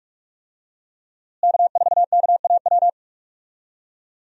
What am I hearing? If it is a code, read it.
Morse code, heard as a keyed tone, about 46 wpm, 700 Hz. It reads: K4KAW
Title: USA Call Signs - 5 Characters - Repeated 2x - 45wpm